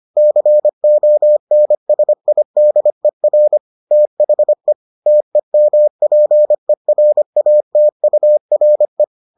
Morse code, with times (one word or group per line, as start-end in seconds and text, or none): 0.2-3.6 CONSIDER
3.9-4.7 THE
5.1-9.0 TEMPERATURE